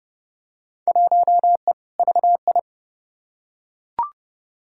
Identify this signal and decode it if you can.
Morse code, heard as a keyed tone, about 30 wpm, 700 Hz. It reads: 1E VI
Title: Field Day Exchanges Encore - 30wpm